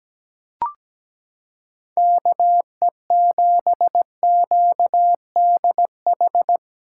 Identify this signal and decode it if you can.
Morse code, heard as a keyed tone, about 17 wpm, 700 Hz. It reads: KE7QDH
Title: USA Callsign DE Callsign - 6 Characters - 17wpm